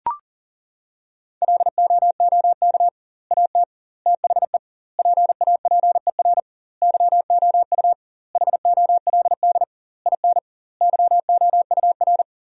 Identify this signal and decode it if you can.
Morse code, heard as a keyed tone, about 40 wpm, 700 Hz. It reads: LOOK AT THE PAPER YOU HOLD IN YOUR
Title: 2x Word-Spacing - Sentences from Top 400 Words - 40wpm